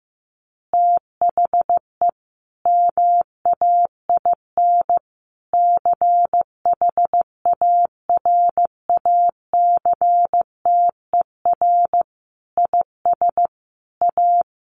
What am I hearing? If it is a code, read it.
Morse code, heard as a keyed tone, about 15 wpm, 700 Hz. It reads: THE MAIN CHARACTER IS A
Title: Additional Easy Phrases 15wpm